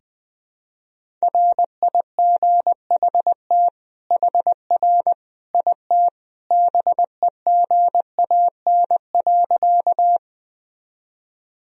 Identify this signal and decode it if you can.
Morse code, heard as a keyed tone, about 20 wpm, 700 Hz. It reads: RIGHT HR IT BEGAN.
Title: Sentences from Top 300 Words and Abbreviations 20wpm